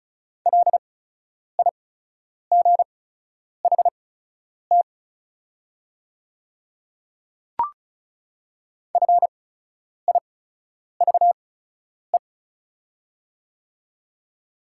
Morse code, heard as a keyed tone, about 36 wpm, spaced as for 10 wpm, 700 Hz. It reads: LIGHT FIVE
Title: 8x Character Spacing - Top 500 Words - 35wpm